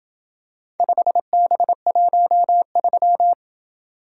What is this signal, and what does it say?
Morse code, heard as a keyed tone, about 26 wpm, 700 Hz. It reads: 5B13